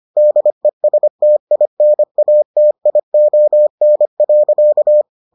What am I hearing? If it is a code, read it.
Morse code, heard as a keyed tone, about 25 wpm, 600 Hz. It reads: DESTINATION.